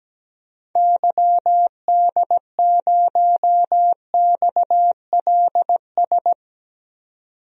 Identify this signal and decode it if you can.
Morse code, heard as a keyed tone, about 17 wpm, 700 Hz. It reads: YD0XLS
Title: DXCC-100 Call Signs - 6 Characters - 17wpm